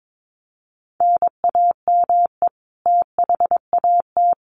Morse code, heard as a keyed tone, about 22 wpm, 700 Hz. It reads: NAME THAT
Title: Sentences from Top 1300 Words 22wpm